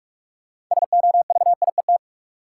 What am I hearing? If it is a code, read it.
Morse code, heard as a keyed tone, about 44 wpm, 700 Hz. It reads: SOVIET
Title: Sets of 5 Words Encore - New Word Drill - 45wpm